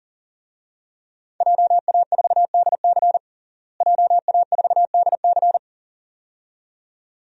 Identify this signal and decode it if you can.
Morse code, heard as a keyed tone, about 40 wpm, 700 Hz. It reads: JA4DC JA4DC
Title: DXCC-100 Call Signs - 5 Characters - Repeated 2x - 40wpm